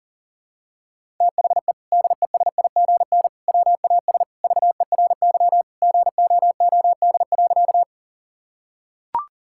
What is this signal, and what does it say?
Morse code, heard as a keyed tone, about 37 wpm, 700 Hz. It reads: THE DESIGN WAS VERY GOOD.
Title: Sentences from Top 1000 Words - Review Part 5 - 40wpm